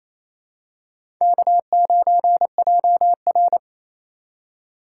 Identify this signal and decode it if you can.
Morse code, heard as a keyed tone, about 28 wpm, 700 Hz. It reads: K9JR
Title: USA Call Signs - 4 Characters - 4x Character Spacing - 28wpm